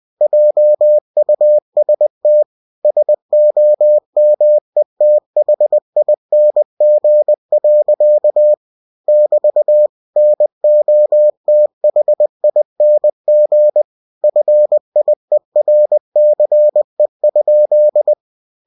Morse code, heard as a keyed tone, about 20 wpm, 600 Hz. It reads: JUST SOMETHING. <BT> NOTHING FIERCE?
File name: cw-books / winnie-the-pooh / winnie-20wpm-page-0079